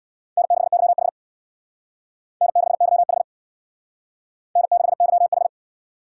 Morse code, heard as a keyed tone, about 64 wpm, 700 Hz. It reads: N6YH N6YH N6YH